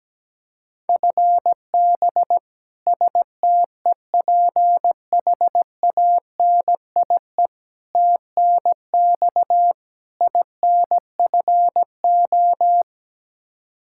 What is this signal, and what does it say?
Morse code, heard as a keyed tone, about 17 wpm, 700 Hz. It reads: FB STEPHANIE TNX INFO
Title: Practice QSO - Exchange 3 - More Info - 17wpm